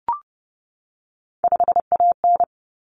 Morse code, heard as a keyed tone, about 30 wpm, 700 Hz. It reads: 5AN
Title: Common Cut Numbers - 3-Digit 30wpm